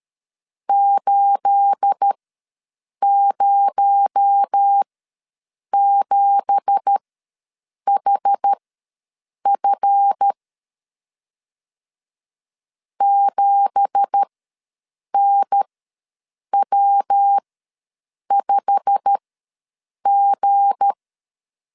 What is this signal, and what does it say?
Morse code, heard as a keyed tone, about 13 wpm, spaced as for 7 wpm, 800 Hz. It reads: 807HF 7NW5G
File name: LEZIONI / 9 2867 ws 12wpm 3431